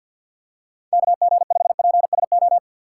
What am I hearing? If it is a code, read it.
Morse code, heard as a keyed tone, about 52 wpm, 700 Hz. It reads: KG5PSO
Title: USA Call Signs - 6 Characters - 6x Character Spacing - 50wpm